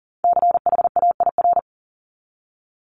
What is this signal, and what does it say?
Morse code, heard as a keyed tone, about 40 wpm, 700 Hz. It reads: CHAIR
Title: Multi-letter Phonemes Repeated 3x 40wpm